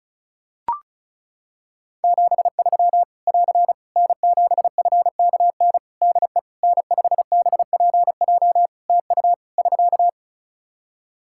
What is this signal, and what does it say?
Morse code, heard as a keyed tone, about 36 wpm, 700 Hz. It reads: 73 <AR> N7FKN DE N5BPJ TU <SK>
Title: Practice QSO - Exchange 4 - Ending - 35wpm